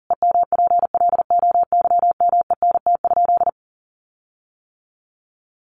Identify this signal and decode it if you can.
Morse code, heard as a keyed tone, about 40 wpm, 700 Hz. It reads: EMPLOYMENT?
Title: Sentences from Top 1500 Words 40wpm